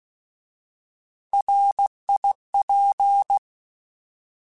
Morse code, heard as a keyed tone, about 16 wpm, 800 Hz. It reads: RIP
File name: Alphabetical / Alphabetical mp3s / Rs @15